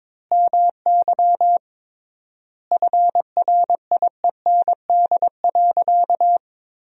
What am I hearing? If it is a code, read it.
Morse code, heard as a keyed tone, about 21 wpm, 700 Hz. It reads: MY FRIEND.
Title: 3x Word-Spacing - Sentences from Top 500 Words - 22wpm